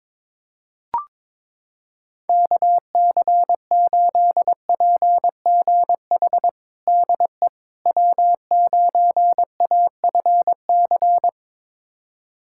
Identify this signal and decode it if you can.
Morse code, heard as a keyed tone, about 22 wpm, 700 Hz. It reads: KC8PGH DE W9AFC
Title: USA Callsign DE Callsign 22wpm